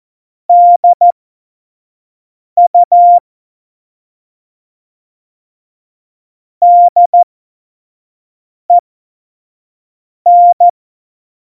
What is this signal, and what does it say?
Morse code, heard as a keyed tone, about 14 wpm, spaced as for 5 wpm, 700 Hz. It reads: DU DEN
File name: Morsekurs / Witze / Witz 29. August 17, Tempo 25 F70